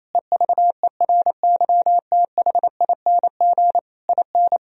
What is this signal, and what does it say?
Morse code, heard as a keyed tone, about 28 wpm, 700 Hz. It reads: EVERYTHING IN
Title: Sentences from Top 1000 Words - Review Part 2 - 28wpm